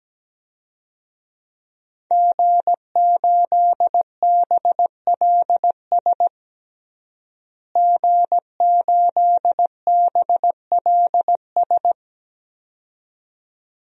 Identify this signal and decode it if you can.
Morse code, heard as a keyed tone, about 17 wpm, 700 Hz. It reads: G8BLS G8BLS
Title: DXCC-100 Call Signs - 5 Characters - Repeated 2x - 17wpm